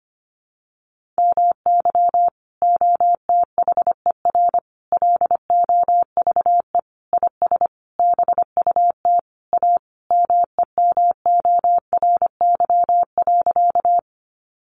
Morse code, heard as a keyed tone, about 25 wpm, 700 Hz. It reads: MY OTHER LOVE IS BUT A MEMORY.
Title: Sentences from Top 1300 Words 25wpm